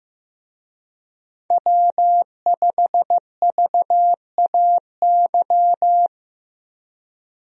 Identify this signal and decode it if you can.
Morse code, heard as a keyed tone, about 15 wpm, 700 Hz. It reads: W5VAY
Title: USA Call Signs - 5 Characters - Repeated 2x - 15wpm